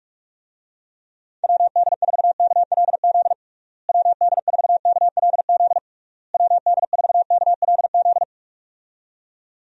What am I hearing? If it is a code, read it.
Morse code, heard as a keyed tone, about 48 wpm, 700 Hz. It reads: WD4KLZ WD4KLZ WD4KLZ
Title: USA Call Signs - 6 Characters - Repeated 3x - 45wpm